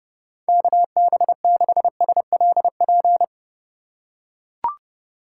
Morse code, heard as a keyed tone, about 30 wpm, 700 Hz. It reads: KB6SLP
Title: USA Call Signs - 6 Characters - 3x Character Spacing - 30wpm